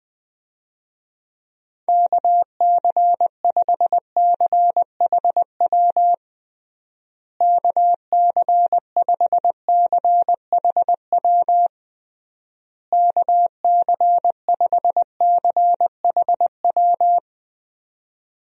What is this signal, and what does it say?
Morse code, heard as a keyed tone, about 20 wpm, 700 Hz. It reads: KC5CHW KC5CHW KC5CHW
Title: USA Call Signs - 6 Characters - Repeated 3x - 20wpm